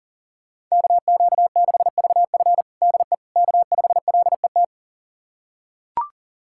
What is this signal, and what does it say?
Morse code, heard as a keyed tone, about 37 wpm, 700 Hz. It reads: KQ6VF DE K5LET
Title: USA Callsign DE Callsign - 5 Characters - 40wpm